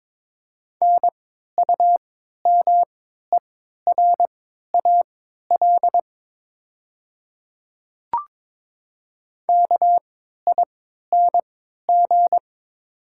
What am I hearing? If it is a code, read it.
Morse code, heard as a keyed tone, about 22 wpm, spaced as for 13 wpm, 690 Hz. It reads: NUMERAL KING
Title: 3x Character Spacing - Top 500 Words - 22wpm